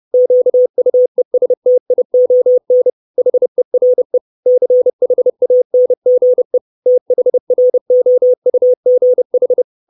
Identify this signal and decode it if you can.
Morse code, heard as a keyed tone, about 30 wpm, 500 Hz. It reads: QUESTION HERE CHANGE THROUGH
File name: 632 Words 30WPM